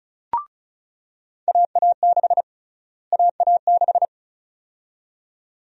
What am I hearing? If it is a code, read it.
Morse code, heard as a keyed tone, about 36 wpm, 700 Hz. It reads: AA6 AA6
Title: USA Call Sign Prefixes - Repeated 2x - 35wpm